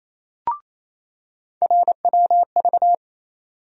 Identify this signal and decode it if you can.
Morse code, heard as a keyed tone, about 27 wpm, 700 Hz. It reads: RWV
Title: USA Call Sign Postfixes - 3 Letters 28wpm